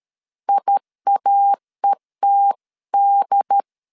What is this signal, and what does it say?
Morse code, heard as a keyed tone, about 13 wpm, 800 Hz. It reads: IAETD
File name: CW / DET / 1 3 1500C S B 12wpm 2828